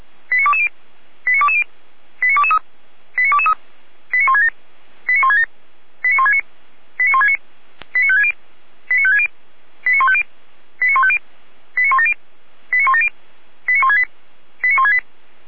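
Five-tone selective call (ZVEI), heard as quick runs of five tones, tones of about 70 ms. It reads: 89220 89220 89223 89223 89167 89167 89169 89169 89560 89560 89150 89150 89170 89170 89168 89168
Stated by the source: ZVEI five-tone call received by SDR.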